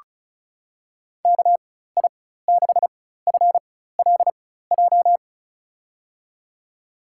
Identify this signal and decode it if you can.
Morse code, heard as a keyed tone, about 36 wpm, spaced as for 16 wpm, 700 Hz. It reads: KI6FLJ
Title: USA Call Signs - 6 Characters - 4x Character Spacing - 35wpm